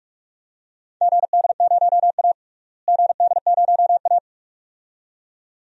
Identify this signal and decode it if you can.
Morse code, heard as a keyed tone, about 42 wpm, 700 Hz. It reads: GD0A GD0A
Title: DXCC-100 Call Signs - 4 Characters - Repeated 2x - 45wpm